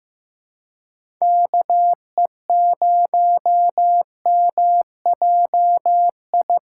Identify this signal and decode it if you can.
Morse code, heard as a keyed tone, about 15 wpm, 700 Hz. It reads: KE0MJI
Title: USA Call Signs - 6 Characters - 2x Character Spacing - 15wpm